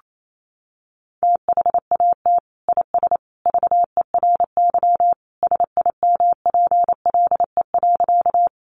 Morse code, heard as a keyed tone, about 28 wpm, 700 Hz. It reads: THAT IS VERY SIMPLE.
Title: Sentences from Top 400 Words 28wpm